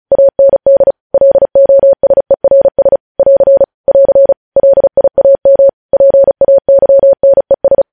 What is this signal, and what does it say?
Morse code, heard as a keyed tone, about 36 wpm, 550 Hz. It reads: AND LOSERS <AR> <AR> LIAM PAYNES
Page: News Headlines in Morse Code at 35 WPM